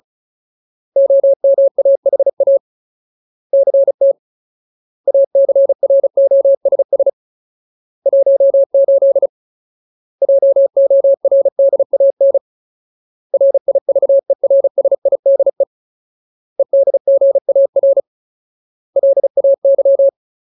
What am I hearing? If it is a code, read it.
Morse code, heard as a keyed tone, about 35 wpm, 550 Hz. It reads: OMAHA CT ACROSS 18 JORDAN RIVERSIDE EDGAR LAY
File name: Ultimate Word List / 35WPM Ultimate